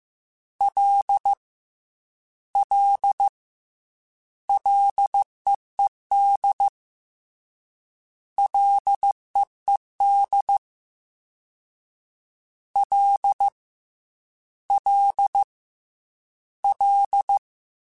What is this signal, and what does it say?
Morse code, heard as a keyed tone, about 15 wpm, 800 Hz. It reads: L L LEED LEED L L L
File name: F and L / F and L mp3 / F and L Prefixes 3x2 Twice @13